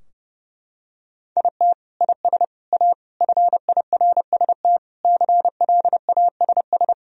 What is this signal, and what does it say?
Morse code, heard as a keyed tone, about 30 wpm, 700 Hz. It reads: IT IS A FIRST CLASS
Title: Sentences from Top 1000 Words - Review Part 2 - 30wpm